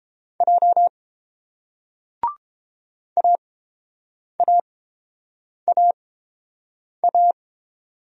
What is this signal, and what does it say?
Morse code, heard as a keyed tone, about 30 wpm, spaced as for 7 wpm, 700 Hz. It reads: J AAAA